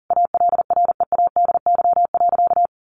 Morse code, heard as a keyed tone, about 40 wpm, 700 Hz. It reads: ALREADY.
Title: Sentences from Top 700 Words 40wpm